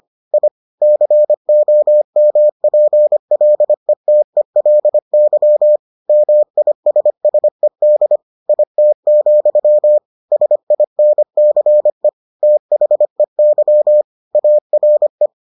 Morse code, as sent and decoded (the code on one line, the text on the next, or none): .. / -.-. --- -- .--. .-.. - . .-.. -.-- / -- .. ... ... -.. / .. - --..-- / ... .. -. -.-. / - .... -.-- / .- .-. .
I COMPLETELY MISSED IT, SINCE THEY ARE